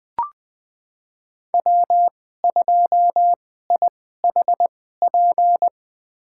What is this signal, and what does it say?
Morse code, heard as a keyed tone, about 20 wpm, spaced as for 15 wpm, 700 Hz. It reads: W2IHP
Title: USA Call Signs - 5 Characters - 2x Character Spacing - 20wpm